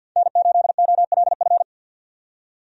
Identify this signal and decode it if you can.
Morse code, heard as a keyed tone, about 50 wpm, 700 Hz. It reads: N8OLF
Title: USA Call Signs - 5 Characters - 50wpm